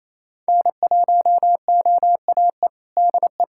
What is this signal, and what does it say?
Morse code, heard as a keyed tone, about 28 wpm, 700 Hz. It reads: N1OAE DE
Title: USA Callsign DE Callsign - 5 Characters - 28wpm